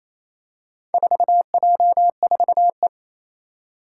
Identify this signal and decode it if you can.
Morse code, heard as a keyed tone, about 27 wpm, 700 Hz. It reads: 4J4E
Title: Four Challenging Letter-Number Combinations 28wpm